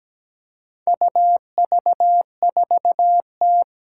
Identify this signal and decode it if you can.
Morse code, heard as a keyed tone, about 17 wpm, 700 Hz. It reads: UV4T
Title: Four Challenging Letter-Number Combinations 17wpm